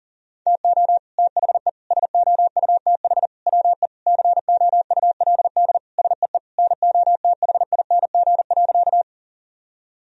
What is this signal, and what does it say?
Morse code, heard as a keyed tone, about 40 wpm, 700 Hz. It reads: TO THE SOUTH WE COULD SEE NOTHING.
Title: Sentences from Top 1000 Words - Review Part 2 - 40wpm